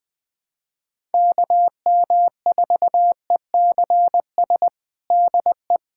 Morse code, heard as a keyed tone, about 20 wpm, 700 Hz. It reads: KM4ECS DE
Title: USA Callsign DE Callsign - 6 Characters - 20wpm